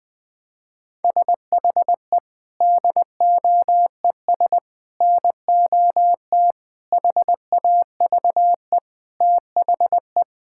Morse code, heard as a keyed tone, about 20 wpm, 700 Hz. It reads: SHE DOES NOT HAVE THE
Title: Sentences with 3-Word Phrases - 20wpm